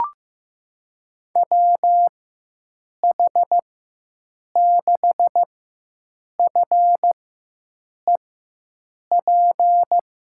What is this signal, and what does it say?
Morse code, heard as a keyed tone, about 15 wpm, spaced as for 7 wpm, 700 Hz. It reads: WH6FEP
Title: USA Call Signs - 4x Character Spacing - 15wpm